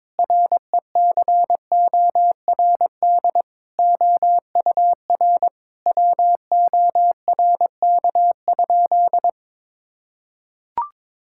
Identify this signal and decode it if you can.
Morse code, heard as a keyed tone, about 22 wpm, 700 Hz. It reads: RECORD OUR WORK?
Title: Sentences from Top 1000 Words - Review Part 6 - 22wpm